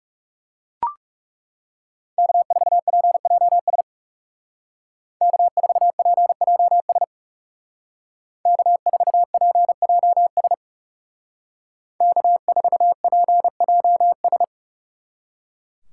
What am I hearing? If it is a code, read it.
Morse code, heard as a keyed tone, about 36 wpm, 700 Hz. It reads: K4PJS K4PJS K4PJS K4PJS
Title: Speed Racing - USA 1x3 Call Signs 30wpm